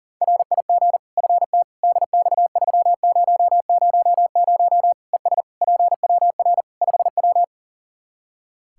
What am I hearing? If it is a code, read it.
Morse code, heard as a keyed tone, about 40 wpm, 700 Hz. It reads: RIG FT DX3000 ES PWR 5W
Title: Practice QSO - Exchange 2 - Info - 40wpm